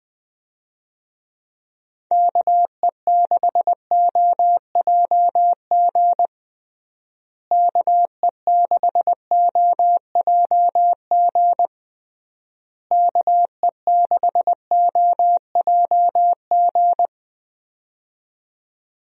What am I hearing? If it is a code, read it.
Morse code, heard as a keyed tone, about 20 wpm, 700 Hz. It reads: KE6OJG KE6OJG KE6OJG
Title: USA Call Signs - 6 Characters - Repeated 3x - 20wpm